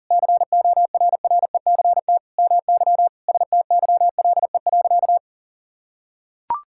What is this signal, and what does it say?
Morse code, heard as a keyed tone, about 40 wpm, 700 Hz. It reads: CORRECT MY STYLE.
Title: Sentences from Top 1000 Words - Review Part 5 - 40wpm